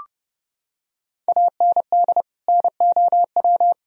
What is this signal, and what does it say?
Morse code, heard as a keyed tone, about 30 wpm, 700 Hz. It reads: AND NOW
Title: Sentences from Top 1000 Words - Review Part 1 - 30wpm